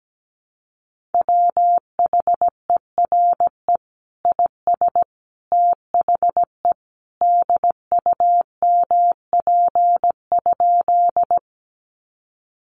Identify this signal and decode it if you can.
Morse code, heard as a keyed tone, about 17 wpm, 700 Hz. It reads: WHERE IS THE DUMP?
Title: Sentences from Top 1300 Words 17wpm